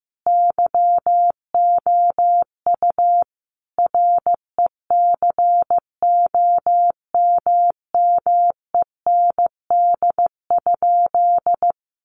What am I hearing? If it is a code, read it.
Morse code, heard as a keyed tone, about 15 wpm, 700 Hz. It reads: YOU RECOMMEND?
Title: Sentences from Top 1200 Words 15wpm